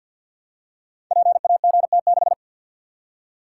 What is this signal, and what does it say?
Morse code, heard as a keyed tone, about 50 wpm, 700 Hz. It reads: PAGT6
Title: Code Groups - 5 Letter:Numbers - 50wpm